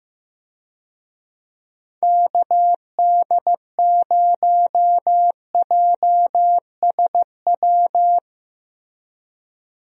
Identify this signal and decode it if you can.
Morse code, heard as a keyed tone, about 15 wpm, 700 Hz. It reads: KD0JSW
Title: USA Call Signs - 6 Characters - Repeated 3x - 15wpm